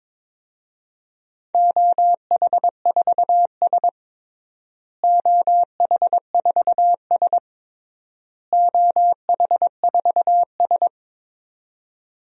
Morse code, heard as a keyed tone, about 22 wpm, 700 Hz. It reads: OH4S OH4S OH4S
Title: DXCC-100 Call Signs - 4 Characters - Repeated 3x - 22wpm